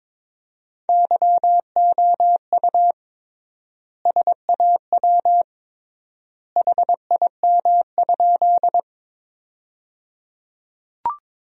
Morse code, heard as a keyed tone, about 22 wpm, 700 Hz. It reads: YOU SAW HIM?
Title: 3x Word-Spacing - Sentences from Top 300 Words - 22wpm